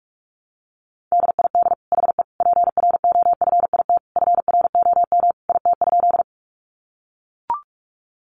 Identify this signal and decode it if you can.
Morse code, heard as a keyed tone, about 44 wpm, 700 Hz. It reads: DID HE PROFIT FROM IT?
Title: Sentences from Top 800 Words 45wpm